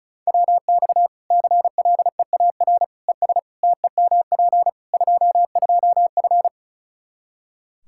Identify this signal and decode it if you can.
Morse code, heard as a keyed tone, about 34 wpm, 700 Hz. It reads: WX CLEAR ES TEMP 22F
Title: Practice QSO - Exchange 2 - Info - 35wpm